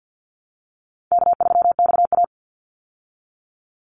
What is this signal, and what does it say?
Morse code, heard as a keyed tone, about 50 wpm, 700 Hz. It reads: K3XA
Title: Call Signs Repeated 2x - 50wpm